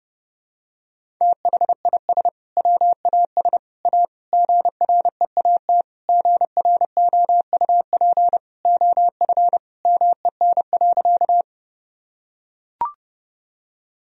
Morse code, heard as a keyed tone, about 30 wpm, 700 Hz. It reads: THIS WAS A GREAT GROUP OF MEN.
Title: Sentences from Top 1000 Words - Review Part 1 - 30wpm